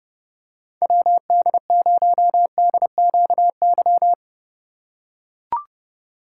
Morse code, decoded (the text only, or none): WD0DQY